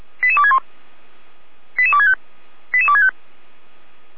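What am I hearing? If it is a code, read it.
Five-tone selective call (ZVEI), heard as quick runs of five tones, tones of about 70 ms. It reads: 88261 88265 88265
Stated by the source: ZVEI five-tone call 88255 received by SDR.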